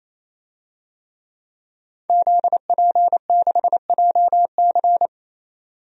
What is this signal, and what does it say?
Morse code, heard as a keyed tone, about 28 wpm, 700 Hz. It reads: ZP6JC
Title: DXCC-100 Call Signs - 5 Characters - 28wpm